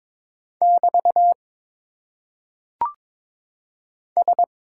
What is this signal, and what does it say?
Morse code, heard as a keyed tone, about 22 wpm, 700 Hz. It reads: <BT> S
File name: Top 90 QSO Elements - 8x Character Spacing - 22wpm